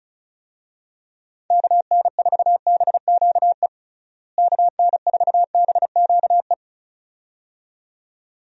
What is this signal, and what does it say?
Morse code, heard as a keyed tone, about 34 wpm, 700 Hz. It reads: KN4BQE KN4BQE
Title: USA Call Signs - 6 Characters - Repeated 2x - 35wpm